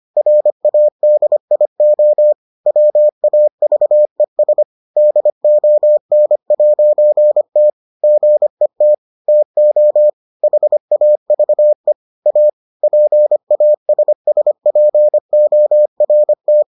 Morse code, heard as a keyed tone, about 25 wpm, 600 Hz. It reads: RADIO WAVES DON'T GET TO HAVE A PASSPORT